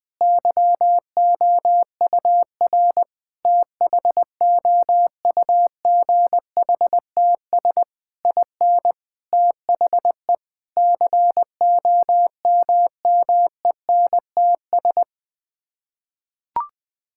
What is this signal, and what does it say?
Morse code, heard as a keyed tone, about 21 wpm, 700 Hz. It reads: YOUR THOUGHTS IN THE COMMENTS
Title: Sets of 8 Words - 20wpm